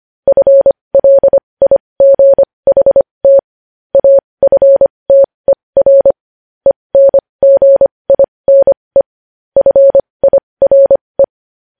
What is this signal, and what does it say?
Morse code, heard as a keyed tone, about 25 wpm, spaced as for 20 wpm, 550 Hz. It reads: FLIGHT AFTER ENGINE FIRE
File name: podcast / News headlines-1x-25x20